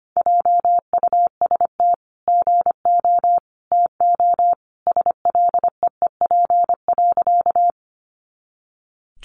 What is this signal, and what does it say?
Morse code, heard as a keyed tone, about 25 wpm, 700 Hz. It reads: JUST GO TO SLEEP.
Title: Sentences from Top 500 Words 25wpm